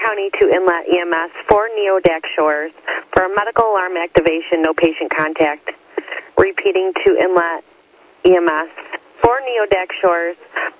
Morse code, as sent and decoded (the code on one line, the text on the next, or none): none